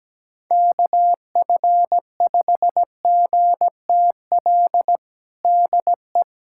Morse code, heard as a keyed tone, about 17 wpm, 700 Hz. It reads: KF5GTL DE
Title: USA Callsign DE Callsign 17wpm